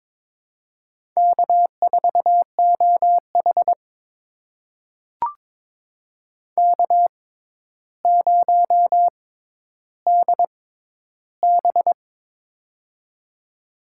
Morse code, heard as a keyed tone, about 22 wpm, spaced as for 11 wpm, 700 Hz. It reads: K4OH K0DB
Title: USA Call Signs - 4 Characters - 6x Character Spacing - 22wpm